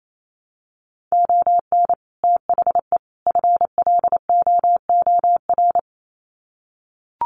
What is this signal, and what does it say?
Morse code, heard as a keyed tone, about 28 wpm, 700 Hz. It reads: ON THE FLOOR
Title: Sets of 3 Words 28wpm